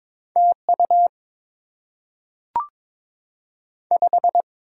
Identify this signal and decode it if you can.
Morse code, heard as a keyed tone, about 22 wpm, 700 Hz. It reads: TU 5
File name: Top 20 QSO Elements - 8x Character Spacing - 22wpm